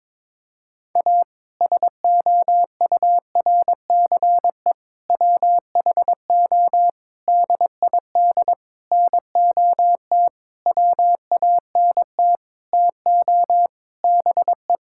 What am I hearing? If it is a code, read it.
Morse code, heard as a keyed tone, about 22 wpm, 700 Hz. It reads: A SOURCE WHO DID NOT WANT TO BE
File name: practice / 22wpm / Sentences with 8-Word Phrases - 22wpm